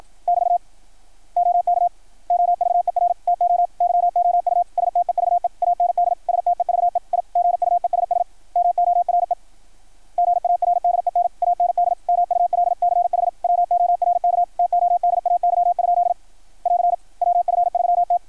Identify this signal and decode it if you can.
Morse code, heard as a keyed tone, about 53 wpm, 660 Hz. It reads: <BT> OK OVER TO YOU STEVE AND STEVE I GUESS MORE GARDEN AND MULCH WORK TODAY? <AR> WF3T
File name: archived / CRSnet